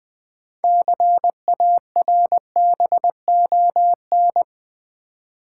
Morse code, heard as a keyed tone, about 20 wpm, 700 Hz. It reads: CARBON
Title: Build A Word - 2 Letters At A Time - 6-Letter Words - 20wpm